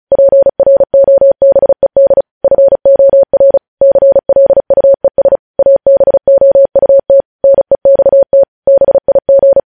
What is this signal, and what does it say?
Morse code, heard as a keyed tone, about 36 wpm, 550 Hz. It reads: PROBED FOR CLUES ABOUT NEXT BIG
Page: News Headlines in Morse Code at 35 WPM